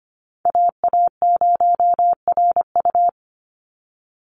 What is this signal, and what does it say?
Morse code, heard as a keyed tone, about 25 wpm, 700 Hz. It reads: AA0RU
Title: USA Call Signs 25wpm